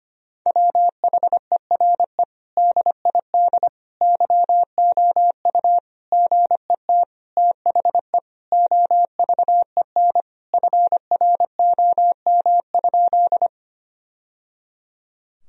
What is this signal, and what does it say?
Morse code, heard as a keyed tone, about 25 wpm, 700 Hz. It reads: WHERE DID YOU GET THE OVEN FROM?
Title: Sentences from Top 1000 Words - Review Part 4 - 25wpm